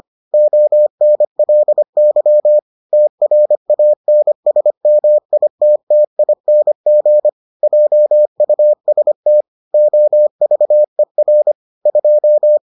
Morse code, heard as a keyed tone, about 25 wpm, 600 Hz. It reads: ONLY TRANSMITTING JUST OVER 2